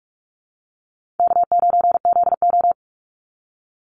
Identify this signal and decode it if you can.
Morse code, heard as a keyed tone, about 46 wpm, 700 Hz. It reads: K9ZO
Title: Easy Call Signs - 45wpm